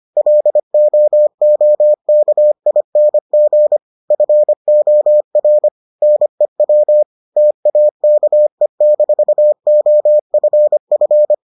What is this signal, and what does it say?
Morse code, heard as a keyed tone, about 25 wpm, 600 Hz. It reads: LOOKING FOR NEW TAKE-OFF